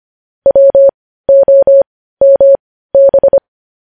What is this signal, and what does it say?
Morse code, heard as a keyed tone, about 25 wpm, spaced as for 15 wpm, 550 Hz. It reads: WOMB
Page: news-headlines-in-morse-code-at-15-wpm